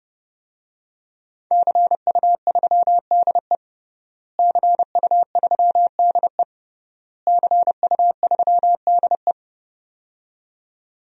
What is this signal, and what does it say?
Morse code, heard as a keyed tone, about 32 wpm, 700 Hz. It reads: CU3DE CU3DE CU3DE
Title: DXCC-100 Call Signs - 5 Characters - Repeated 3x - 30wpm